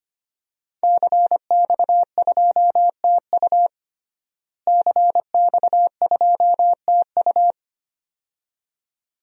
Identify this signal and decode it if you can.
Morse code, heard as a keyed tone, about 25 wpm, 700 Hz. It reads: CX2TU CX2TU
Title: DXCC-100 Call Signs - 5 Characters - Repeated 2x - 25wpm